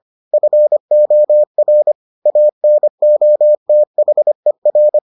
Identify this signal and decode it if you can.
Morse code, heard as a keyed tone, about 25 wpm, 600 Hz. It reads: FOR ANOTHER